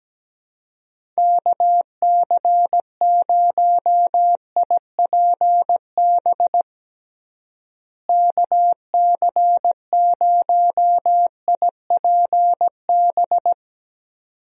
Morse code, heard as a keyed tone, about 17 wpm, 700 Hz. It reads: KC0IPB KC0IPB
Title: USA Call Signs - 6 Characters - Repeated 2x - 17wpm